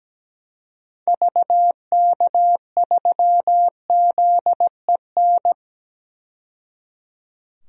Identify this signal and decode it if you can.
Morse code, heard as a keyed tone, about 17 wpm, 700 Hz. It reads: VK3ZEN
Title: DXCC-100 Call Signs - 6 Characters - 17wpm